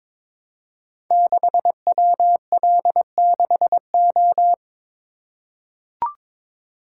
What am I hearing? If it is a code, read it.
Morse code, heard as a keyed tone, about 22 wpm, 700 Hz. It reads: 6WL6O
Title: Code Groups - 5 Letter:Numbers - 22wpm